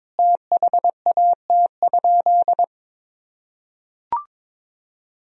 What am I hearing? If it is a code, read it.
Morse code, heard as a keyed tone, about 22 wpm, 700 Hz. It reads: THAT?
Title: Sentences from Top 100 Words and Abbreviations 22wpm